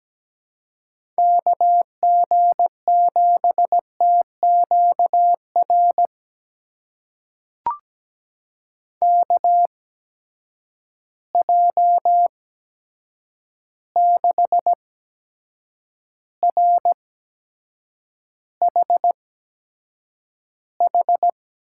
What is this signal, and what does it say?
Morse code, heard as a keyed tone, about 17 wpm, spaced as for 7 wpm, 700 Hz. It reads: KG7TQR KJ6RHH
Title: USA Call Signs - 6 Characters - 8x Character Spacing - 17wpm